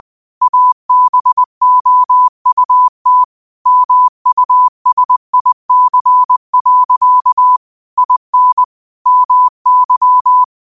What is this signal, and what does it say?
Morse code, heard as a keyed tone, about 20 wpm, 1000 Hz. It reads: ABOUT MUSIC. IN MY